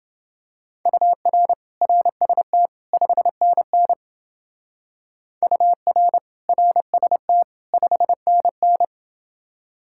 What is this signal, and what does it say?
Morse code, heard as a keyed tone, about 28 wpm, 700 Hz. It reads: UR RST 5NN UR RST 5NN